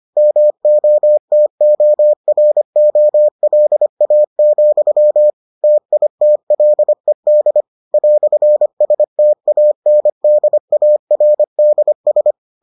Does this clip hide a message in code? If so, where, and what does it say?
Yes: Morse code from 0.2 to 12.3 s, heard as a keyed tone, about 25 wpm, 600 Hz. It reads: MOTOROLA, TITLED "STANDARDS